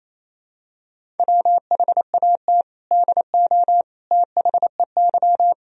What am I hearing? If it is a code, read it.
Morse code, heard as a keyed tone, about 28 wpm, 700 Hz. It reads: WHAT DO THEY
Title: Sentences from Top 1000 Words - Review Part 1 - 28wpm